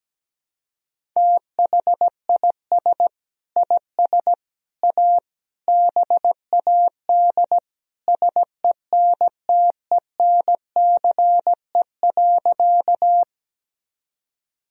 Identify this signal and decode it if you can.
Morse code, heard as a keyed tone, about 17 wpm, 700 Hz. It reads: THIS IS A BAD SENTENCE.
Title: Sentences from Top 1000 Words - Review Part 1 - 17wpm